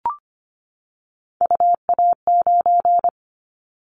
Morse code, heard as a keyed tone, about 25 wpm, 700 Hz. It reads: UA9
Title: Callsign Prefixes for Top 100 DXCC - 25wpm